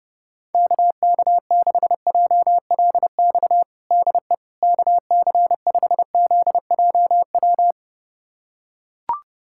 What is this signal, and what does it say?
Morse code, heard as a keyed tone, about 30 wpm, 700 Hz. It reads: KK6JLX DE KC5ZJW
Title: USA Callsign DE Callsign 30wpm